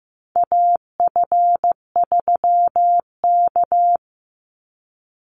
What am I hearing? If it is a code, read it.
Morse code, heard as a keyed tone, about 15 wpm, 700 Hz. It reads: AF3K
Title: Call Signs Repeated 3x - 15wpm